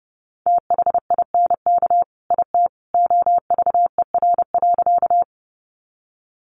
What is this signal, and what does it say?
Morse code, heard as a keyed tone, about 30 wpm, 700 Hz. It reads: THINK IT OVER.
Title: Sentences from Top 200 Words 30wpm